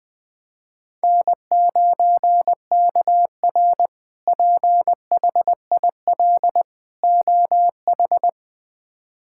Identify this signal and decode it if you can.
Morse code, heard as a keyed tone, about 20 wpm, 700 Hz. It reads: N9KR PHIL OH
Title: K1USN SST QSO Practice - 20wpm